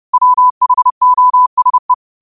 Morse code, heard as a keyed tone, about 30 wpm, 1000 Hz. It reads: WHOSE
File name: qotdfiles / 30wpm